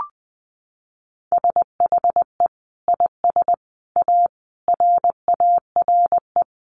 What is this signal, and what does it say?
Morse code, heard as a keyed tone, about 20 wpm, 700 Hz. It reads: SHE IS A RARE
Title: Sentences from Top 1100 Words 20wpm